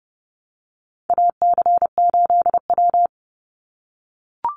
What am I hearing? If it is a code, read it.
Morse code, heard as a keyed tone, about 30 wpm, 700 Hz. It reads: AC8W
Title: Call Signs - 30wpm